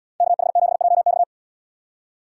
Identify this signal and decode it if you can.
Morse code, heard as a keyed tone, about 77 wpm, 700 Hz. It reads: KH7JX